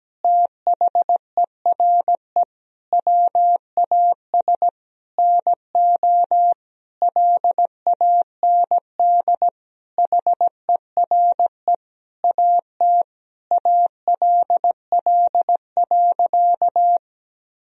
Morse code, heard as a keyed tone, about 17 wpm, 700 Hz. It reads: THERE WAS NO LAND HERE AT ALL.
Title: Sentences from Top 1000 Words - Review Part 1 - 17wpm